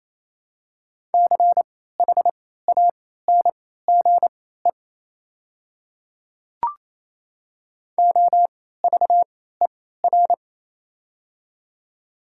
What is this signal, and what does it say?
Morse code, heard as a keyed tone, about 28 wpm, spaced as for 16 wpm, 700 Hz. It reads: CHANGE OVER
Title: 3x Character Spacing - Top 200 Words - 28wpm